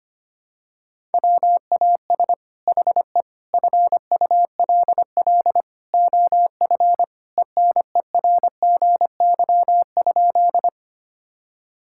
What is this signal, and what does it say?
Morse code, heard as a keyed tone, about 25 wpm, 700 Hz. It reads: WAS HE FULL OF ENERGY?
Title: Sentences from Top 1000 Words - Review Part 6 - 25wpm